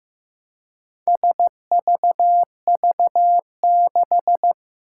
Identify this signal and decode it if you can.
Morse code, heard as a keyed tone, about 15 wpm, 700 Hz. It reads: SVV6
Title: Four Challenging Letter-Number Combinations 15wpm